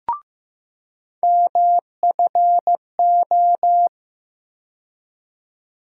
Morse code, heard as a keyed tone, about 15 wpm, 700 Hz. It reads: MFO
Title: USA Call Sign Postfixes - 3 Letters 15wpm